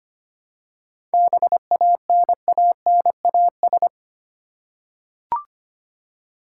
Morse code, heard as a keyed tone, about 25 wpm, 700 Hz. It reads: BANANAS
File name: practice / 25wpm / Easy Words with Hints - 25wpm